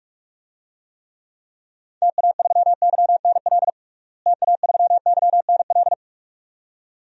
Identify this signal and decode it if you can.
Morse code, heard as a keyed tone, about 42 wpm, 700 Hz. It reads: TA3YNL TA3YNL
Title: DXCC-100 Call Signs - 6 Characters - Repeated 2x - 45wpm